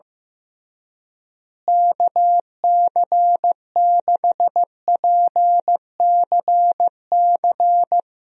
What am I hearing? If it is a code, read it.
Morse code, heard as a keyed tone, about 15 wpm, 700 Hz. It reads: KC6PCC
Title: USA Call Signs - 6 Characters - Repeated 3x - 15wpm